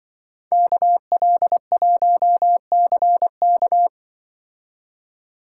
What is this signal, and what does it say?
Morse code, heard as a keyed tone, about 23 wpm, 700 Hz. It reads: KL1CK
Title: Speed Racing - USA 2x2 Call Signs 20wpm